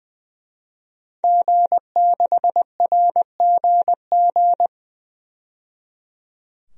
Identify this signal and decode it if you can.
Morse code, heard as a keyed tone, about 21 wpm, 700 Hz. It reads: G6RGG
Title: DXCC-100 Call Signs - 5 Characters - 20wpm